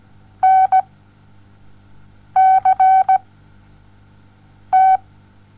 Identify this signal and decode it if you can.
Morse code, heard as a keyed tone, about 17 wpm, 760 Hz. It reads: N C T
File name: Onlinelehrgang / morsen / DJ4UF Morsen07